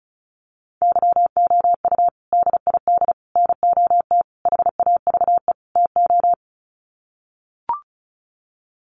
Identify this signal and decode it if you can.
Morse code, heard as a keyed tone, about 34 wpm, 700 Hz. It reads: YOU DID NOT HAVE TO
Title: Sets of 5 Words Encore 35wpm